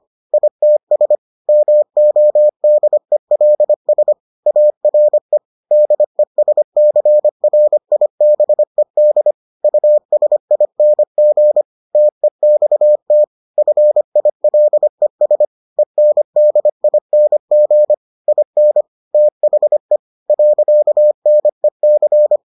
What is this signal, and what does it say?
Morse code, heard as a keyed tone, about 25 wpm, 600 Hz. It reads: ITS MODELS ARE DESCRIBED USING TEXT FILES ENDING IN THE .NEC